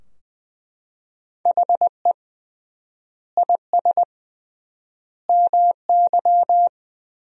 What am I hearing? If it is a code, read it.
Morse code, heard as a keyed tone, about 20 wpm, 700 Hz. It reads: HE IS MY
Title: 3x Word-Spacing - Sentences from Top 300 Words - 20wpm